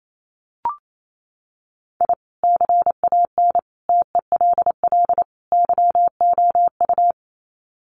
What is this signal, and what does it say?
Morse code, heard as a keyed tone, about 28 wpm, 700 Hz. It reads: I CAN TELL YOU
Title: Sets of 4 Words 28wpm